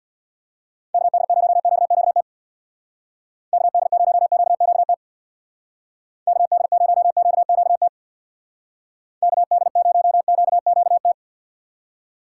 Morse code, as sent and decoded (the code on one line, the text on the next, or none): -.- -.. ----- --.- --.- - / -.- -.. ----- --.- --.- - / -.- -.. ----- --.- --.- - / -.- -.. ----- --.- --.- -
KD0QQT KD0QQT KD0QQT KD0QQT